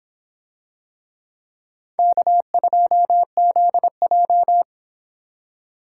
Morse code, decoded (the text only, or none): K2ZJ